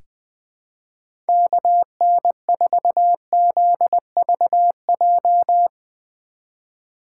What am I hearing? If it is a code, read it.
Morse code, heard as a keyed tone, about 20 wpm, 700 Hz. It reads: KN4ZVJ
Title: 2023 POTA Hunter Call Signs - 20wpm